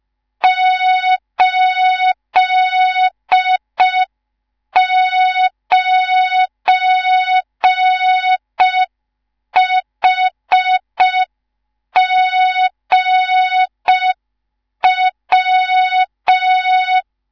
Morse code, heard as a keyed tone, about 5 wpm, 3700 Hz. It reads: 89HGW
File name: Morze-HA2MN / hang / 19 GYAKORLAT